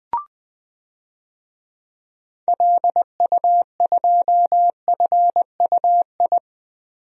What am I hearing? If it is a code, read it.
Morse code, heard as a keyed tone, about 20 wpm, 700 Hz. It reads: LU2FUI
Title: DXCC-100 Call Signs - 6 Characters - 20wpm